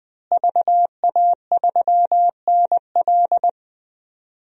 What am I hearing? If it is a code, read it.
Morse code, heard as a keyed tone, about 20 wpm, 700 Hz. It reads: VA3NL
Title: DXCC-100 Call Signs - 5 Characters - Repeated 3x - 20wpm